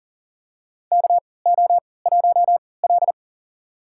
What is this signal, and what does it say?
Morse code, heard as a keyed tone, about 40 wpm, spaced as for 23 wpm, 700 Hz. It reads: KO1L
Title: USA Call Signs - 4 Characters - 3x Character Spacing - 40wpm